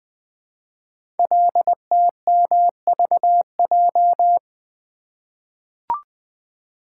Morse code, heard as a keyed tone, about 20 wpm, 700 Hz. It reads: LTMVJ
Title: Code Groups - 5 Letters - 20wpm